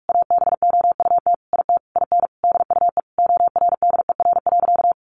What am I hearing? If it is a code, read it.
Morse code, heard as a keyed tone, about 44 wpm, 700 Hz. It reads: ABOUT IT IN DUE ORDER.
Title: Sentences from Top 900 Words 45wpm